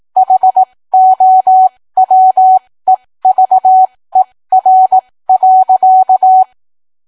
Morse code, heard as a keyed tone, about 18 wpm, spaced as for 15 wpm, 750 Hz. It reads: HOWEVER.